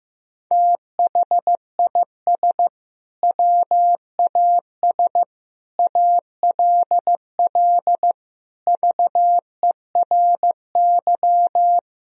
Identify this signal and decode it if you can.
Morse code, heard as a keyed tone, about 16 wpm, 700 Hz. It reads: THIS WAS ALL VERY
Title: Sentences from Top 1000 Words - Review Part 3 - 15wpm